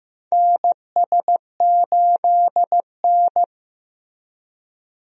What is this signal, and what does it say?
Morse code, heard as a keyed tone, about 16 wpm, 700 Hz. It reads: NS8N
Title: USA Call Signs - 4 Characters - Repeated 3x - 15wpm